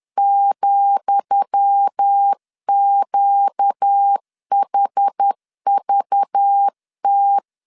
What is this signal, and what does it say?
Morse code, heard as a keyed tone, about 10 wpm, 800 Hz. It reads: ,QHVT
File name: LEZIONI / 10 SIGNS / 10D COMA 10wpm 3013